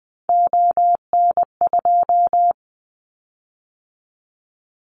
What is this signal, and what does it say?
Morse code, heard as a keyed tone, about 20 wpm, 700 Hz. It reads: ON2
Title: Callsign Prefixes for Top 50 DXCC - 20wpm